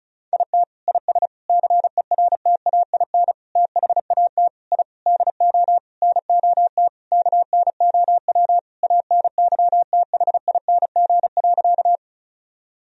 Morse code, heard as a keyed tone, about 36 wpm, 700 Hz. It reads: IT IS CERTAIN THAT I DO NOT KNOW ANYTHING.
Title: Sentences from Top 1000 Words - Review Part 2 - 35wpm